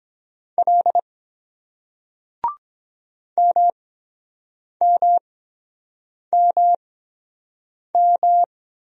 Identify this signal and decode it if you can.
Morse code, heard as a keyed tone, about 23 wpm, spaced as for 7 wpm, 700 Hz. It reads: L MMMM